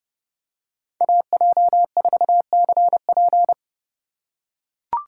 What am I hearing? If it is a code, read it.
Morse code, heard as a keyed tone, about 30 wpm, 700 Hz. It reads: AJ4CP
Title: USA Call Signs - 5 Characters - 4x Character Spacing - 30wpm